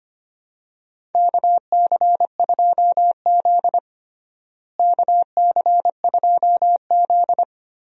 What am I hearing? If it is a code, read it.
Morse code, heard as a keyed tone, about 25 wpm, 700 Hz. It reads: KC2Z KC2Z
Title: USA Call Signs - 4 Characters - Repeated 2x - 25wpm